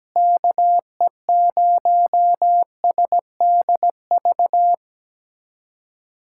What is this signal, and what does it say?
Morse code, heard as a keyed tone, about 17 wpm, 700 Hz. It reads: KE0SDV
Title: USA Callsign DE Callsign 17wpm